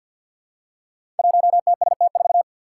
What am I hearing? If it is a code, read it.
Morse code, heard as a keyed tone, about 53 wpm, 700 Hz. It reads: 1TST4
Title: Five Challenging Letter-Number Combinations 50wpm